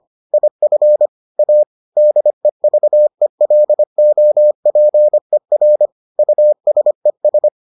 Morse code, heard as a keyed tone, about 25 wpm, 600 Hz. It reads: IF A DEVELOPER USES